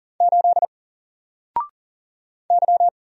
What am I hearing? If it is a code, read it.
Morse code, heard as a keyed tone, about 40 wpm, 700 Hz. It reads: 8 Y
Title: Single Letter-Number Alt 40wpm